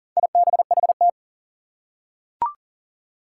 Morse code, heard as a keyed tone, about 40 wpm, 700 Hz. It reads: IBHT
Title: Four Challenging Letter-Number Combinations 40wpm